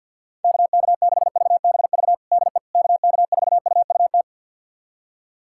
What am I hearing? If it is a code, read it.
Morse code, heard as a keyed tone, about 50 wpm, 700 Hz. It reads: KK6VBV DE KK4UUT
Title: USA Callsign DE Callsign - 6 Characters - 50wpm